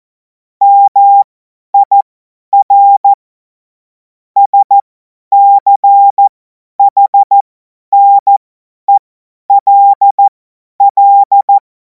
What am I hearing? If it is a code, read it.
Morse code, heard as a keyed tone, about 14 wpm, spaced as for 10 wpm, 800 Hz. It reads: MIR SCHNELL
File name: Morsekurs / Witze / Witz 31. März 2026 50 F70